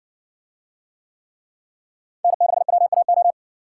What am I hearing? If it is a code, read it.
Morse code, heard as a keyed tone, about 60 wpm, 700 Hz. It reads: N6WAO